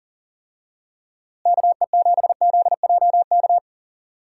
Morse code, heard as a keyed tone, about 40 wpm, 700 Hz. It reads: KE7ZJK
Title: USA Call Signs - 6 Characters - Repeated 2x - 40wpm